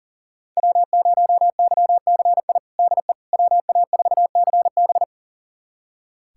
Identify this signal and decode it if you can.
Morse code, heard as a keyed tone, about 40 wpm, 700 Hz. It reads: W0YCI DE WA4CB
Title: USA Callsign DE Callsign - 5 Characters - 40wpm